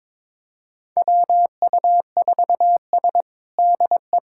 Morse code, heard as a keyed tone, about 22 wpm, 700 Hz. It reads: WU4S DE
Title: USA Callsign DE Callsign - 4 Characters - 22wpm